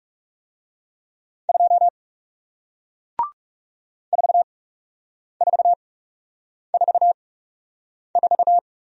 Morse code, heard as a keyed tone, about 41 wpm, spaced as for 8 wpm, 700 Hz. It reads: 2 4444